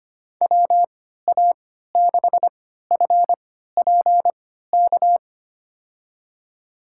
Morse code, heard as a keyed tone, about 25 wpm, spaced as for 14 wpm, 700 Hz. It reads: WA6FPK
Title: USA Call Signs - 6 Characters - 3x Character Spacing - 25wpm